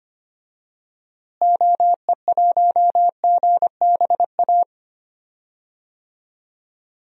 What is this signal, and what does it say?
Morse code, heard as a keyed tone, about 25 wpm, 700 Hz. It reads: OE1GBA
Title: DXCC-100 Call Signs - 6 Characters - Repeated 3x - 25wpm